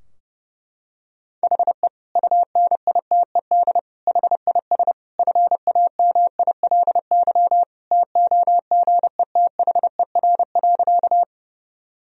Morse code, heard as a keyed tone, about 30 wpm, 700 Hz. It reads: HE UNITED HIS FAMILY TOGETHER.
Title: Sentences from Top 1000 Words - Review Part 4 - 30wpm